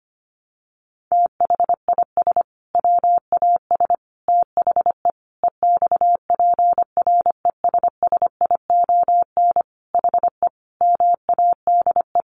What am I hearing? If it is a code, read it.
Morse code, heard as a keyed tone, about 25 wpm, 700 Hz. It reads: THIS WAS THE EXPRESSION HE MADE
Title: Sentences from Top 1200 Words 25wpm